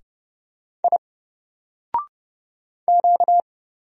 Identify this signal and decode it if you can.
Morse code, heard as a keyed tone, about 30 wpm, 700 Hz. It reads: I Q
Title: Single Letter-Number Alt 30wpm